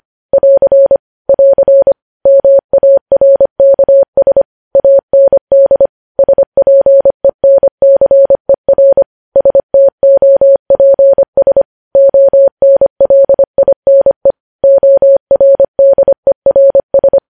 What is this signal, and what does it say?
Morse code, heard as a keyed tone, about 25 wpm, 550 Hz. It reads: <AR> <AR> MARKS AND SPENCER STOPS ONLINE ORDERS
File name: podcast / News headlines-1x-25x25